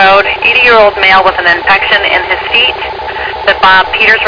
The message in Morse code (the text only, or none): none